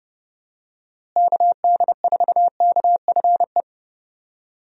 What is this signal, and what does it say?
Morse code, heard as a keyed tone, about 30 wpm, 700 Hz. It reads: KD4KFE
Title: USA Call Signs - 6 Characters - Repeated 3x - 30wpm